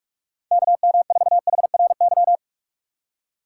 Morse code, heard as a keyed tone, about 46 wpm, 700 Hz. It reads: KM4HRY
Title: USA Call Signs - 4x Character Spacing - 45wpm